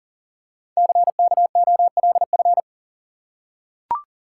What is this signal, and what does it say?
Morse code, heard as a keyed tone, about 42 wpm, 700 Hz. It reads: CKOLF
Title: Code Groups - 5 Letters - 40wpm